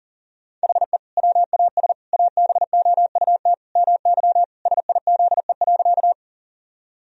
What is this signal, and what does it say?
Morse code, heard as a keyed tone, about 40 wpm, 700 Hz. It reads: HE WAS ABOUT MY SIZE.
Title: Sentences from Top 1000 Words - Review Part 2 - 40wpm